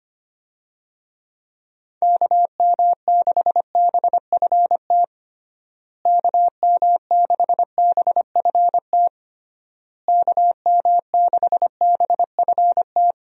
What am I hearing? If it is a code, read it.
Morse code, heard as a keyed tone, about 25 wpm, 700 Hz. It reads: KM6BFT KM6BFT KM6BFT
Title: USA Call Signs - 6 Characters - Repeated 3x - 25wpm